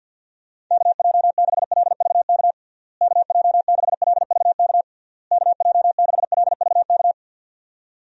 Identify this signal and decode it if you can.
Morse code, heard as a keyed tone, about 47 wpm, 700 Hz. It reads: KJ6LVK KJ6LVK KJ6LVK
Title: USA Call Signs - 6 Characters - Repeated 3x - 50wpm